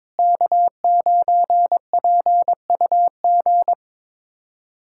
Morse code, heard as a keyed tone, about 21 wpm, 700 Hz. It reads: K9PUG
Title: USA Callsign DE Callsign 22wpm